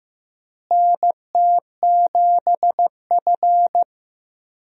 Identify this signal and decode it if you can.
Morse code, heard as a keyed tone, about 15 wpm, 700 Hz. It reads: NT7F
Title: USA Call Signs - 4 Characters - Repeated 3x - 15wpm